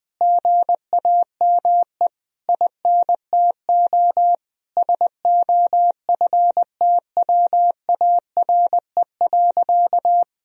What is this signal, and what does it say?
Morse code, heard as a keyed tone, about 21 wpm, 700 Hz. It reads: GAME INTO SOFTWARE.
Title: Sentences from Top 1000 Words - Review Part 3 - 20wpm